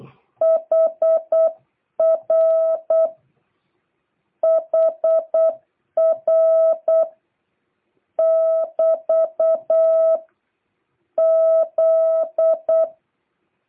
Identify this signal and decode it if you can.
Morse code, heard as a keyed tone, about 8 wpm, 650 Hz. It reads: HR HR <BT> Z